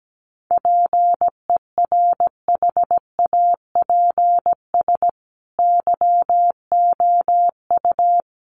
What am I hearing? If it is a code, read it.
Morse code, heard as a keyed tone, about 17 wpm, 700 Hz. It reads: PERHAPS YOU
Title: Sentences from Top 1200 Words 17wpm